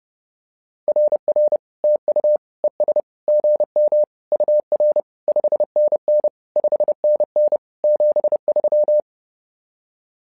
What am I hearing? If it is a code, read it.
Morse code, heard as a keyed tone, about 30 wpm, 600 Hz. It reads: RR TU ES GM UR 5NN 5NN 73
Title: SOTA QSO Practice 30wpm